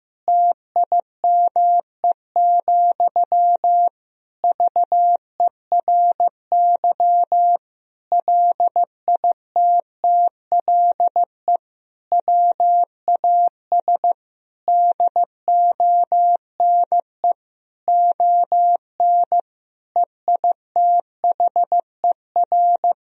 Morse code, heard as a keyed tone, about 16 wpm, 700 Hz. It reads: TIME, VERY LITTLE WAS DONE ON EITHER
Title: Sentences from Top 1000 Words - Review Part 4 - 15wpm